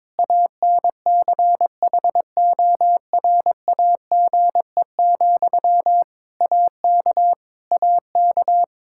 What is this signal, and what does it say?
Morse code, heard as a keyed tone, about 22 wpm, 700 Hz. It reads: ANCHORAGE, AK AK
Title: Practice QSO - Exchange 1 - Beginning - 22wpm